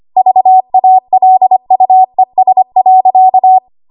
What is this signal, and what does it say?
Morse code, heard as a keyed tone, about 25 wpm, 750 Hz. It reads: VALUES.
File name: Archive / 25 WPM / 25WPM